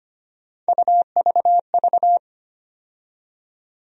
Morse code, heard as a keyed tone, about 25 wpm, 700 Hz. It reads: UVV
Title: USA Call Sign Postfixes - 3 Letters 25wpm